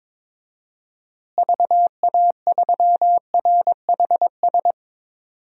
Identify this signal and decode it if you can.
Morse code, heard as a keyed tone, about 22 wpm, 700 Hz. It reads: VA3RHS
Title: DXCC-100 Call Signs - 6 Characters - Repeated 2x - 22wpm